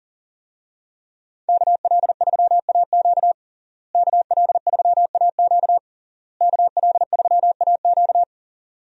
Morse code, heard as a keyed tone, about 40 wpm, 700 Hz. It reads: KL3AQ KL3AQ KL3AQ
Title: USA Call Signs - 5 Characters - Repeated 3x - 40wpm